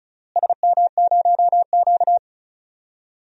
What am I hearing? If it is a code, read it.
Morse code, heard as a keyed tone, about 34 wpm, 700 Hz. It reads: SM0Q
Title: DXCC-100 Call Signs - 4 Characters - Repeated 3x - 35wpm